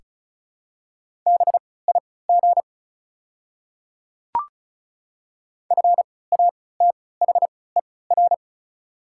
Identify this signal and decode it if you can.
Morse code, heard as a keyed tone, about 36 wpm, spaced as for 20 wpm, 700 Hz. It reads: BIG FATHER
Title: 3x Character Spacing - Top 200 Words - 35wpm